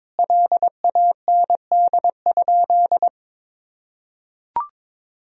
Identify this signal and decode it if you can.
Morse code, heard as a keyed tone, about 22 wpm, 700 Hz. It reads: LAND?
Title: Sentences from Top 1000 Words - Review Part 3 - 22wpm